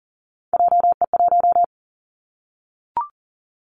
Morse code, heard as a keyed tone, about 41 wpm, 700 Hz. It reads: JE1
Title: Callsign Prefixes for Top 25 DXCC - 40wpm